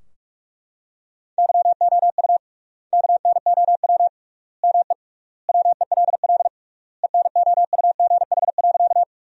Morse code, heard as a keyed tone, about 43 wpm, 700 Hz. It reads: YOU KNOW ME WELL ENOUGH.
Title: 3x Word-Spacing - Sentences from Top 400 Words - 45wpm